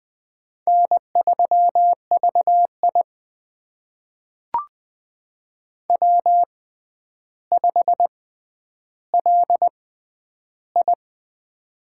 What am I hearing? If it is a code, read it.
Morse code, heard as a keyed tone, about 20 wpm, spaced as for 10 wpm, 700 Hz. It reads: N3VI W5LI
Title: USA Call Signs - 4 Characters - 6x Character Spacing - 20wpm